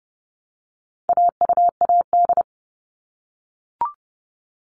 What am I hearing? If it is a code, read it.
Morse code, heard as a keyed tone, about 30 wpm, 700 Hz. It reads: AUAD
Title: All Cut Numbers - 4-Digit 30wpm